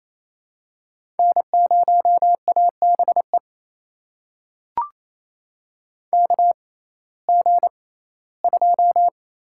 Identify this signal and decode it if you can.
Morse code, heard as a keyed tone, about 28 wpm, spaced as for 17 wpm, 700 Hz. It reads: N0ABE KG2
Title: USA Call Signs - 5 Characters - 6x Character Spacing - 28wpm